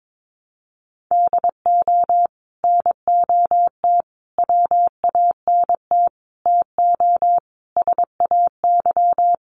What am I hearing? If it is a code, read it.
Morse code, heard as a keyed tone, about 22 wpm, 700 Hz. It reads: DO NOT WANT TO SAY
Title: Sets of 5 Words Encore 22wpm